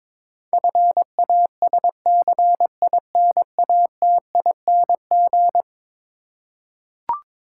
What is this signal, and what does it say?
Morse code, heard as a keyed tone, about 22 wpm, 700 Hz. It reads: FASCINATING
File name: practice / 22wpm / Eleven Letter Words - 22wpm